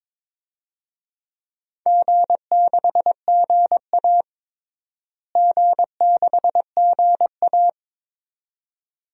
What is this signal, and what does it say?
Morse code, heard as a keyed tone, about 22 wpm, 700 Hz. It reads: G6GA G6GA
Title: DXCC-100 Call Signs - 4 Characters - Repeated 2x - 22wpm